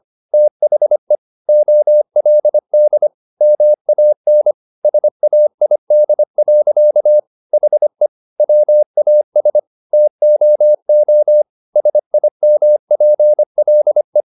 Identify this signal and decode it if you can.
Morse code, heard as a keyed tone, about 25 wpm, 600 Hz. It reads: THE OLD MAN SAID. HE WAS TOO SIMPLE
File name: the old man and the sea part 1 25wpm